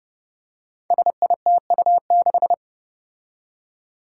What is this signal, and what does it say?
Morse code, heard as a keyed tone, about 30 wpm, 700 Hz. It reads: SITU6
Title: Five Challenging Letter-Number Combinations 30wpm